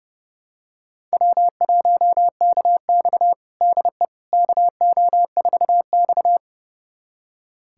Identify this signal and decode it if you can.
Morse code, heard as a keyed tone, about 32 wpm, 700 Hz. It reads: W1KX DE KO4X
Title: USA Callsign DE Callsign - 4 Characters - 30wpm